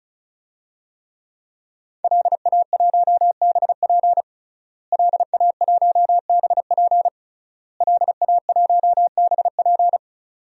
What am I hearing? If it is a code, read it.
Morse code, heard as a keyed tone, about 34 wpm, 700 Hz. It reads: LA1BP LA1BP LA1BP
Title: DXCC-100 Call Signs - 5 Characters - Repeated 3x - 35wpm